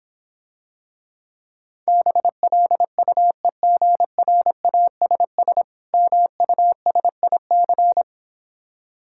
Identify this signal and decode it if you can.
Morse code, heard as a keyed tone, about 26 wpm, 700 Hz. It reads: BLUEGRASS MUSIC